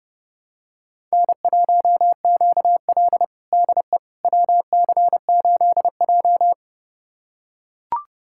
Morse code, heard as a keyed tone, about 32 wpm, 700 Hz. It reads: N1QL DE WC8J
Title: USA Callsign DE Callsign - 4 Characters - 30wpm